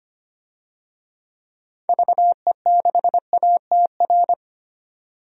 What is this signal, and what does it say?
Morse code, heard as a keyed tone, about 25 wpm, 700 Hz. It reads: VE6ATR
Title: DXCC-100 Call Signs - 6 Characters - 25wpm